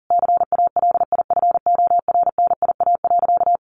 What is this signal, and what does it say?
Morse code, heard as a keyed tone, about 40 wpm, 700 Hz. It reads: CALIFORNIA.
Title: Easy Phrases 40wpm